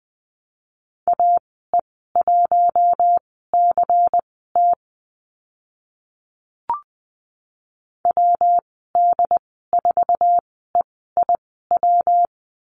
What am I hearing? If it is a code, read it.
Morse code, heard as a keyed tone, about 20 wpm, spaced as for 14 wpm, 700 Hz. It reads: AE1CT WD4EIW
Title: 140 Lesson 23g - Letter f - Call Signs 20wpm